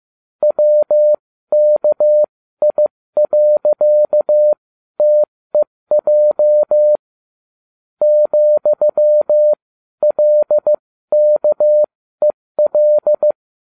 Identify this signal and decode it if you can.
Morse code, heard as a keyed tone, about 15 wpm, spaced as for 13 wpm, 600 Hz. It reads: WKI.TEJ ,LKEL